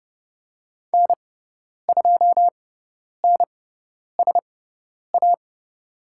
Morse code, heard as a keyed tone, about 30 wpm, spaced as for 10 wpm, 700 Hz. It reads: N2NSA
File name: Lesson 16g - Letter c - Call Signs 30wpm